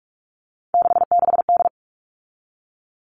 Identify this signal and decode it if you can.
Morse code, heard as a keyed tone, about 44 wpm, 700 Hz. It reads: 66D
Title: All Cut Numbers - 3-Digit 45wpm